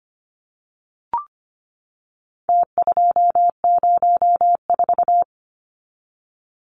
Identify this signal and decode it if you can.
Morse code, heard as a keyed tone, about 25 wpm, 700 Hz. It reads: T204
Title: Numbers - Contest Style - 4-Digit 25wpm